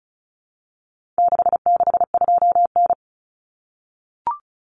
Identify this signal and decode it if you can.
Morse code, heard as a keyed tone, about 34 wpm, 700 Hz. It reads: 662N
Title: Common Cut Numbers - 4-Digit 35wpm